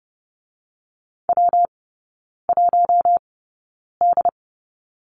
Morse code, heard as a keyed tone, about 30 wpm, spaced as for 9 wpm, 700 Hz. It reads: W1D
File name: Lesson 19e - Letter w - Three Characters 30wpm